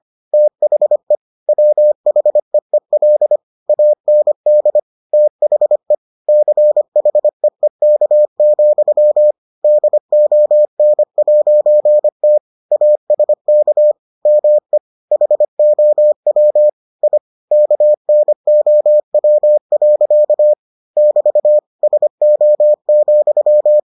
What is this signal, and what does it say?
Morse code, heard as a keyed tone, about 25 wpm, 600 Hz. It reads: THE WHEEL AND THE CHEEK, DON'T ASK ME HOW I KNOW. <BT> SO,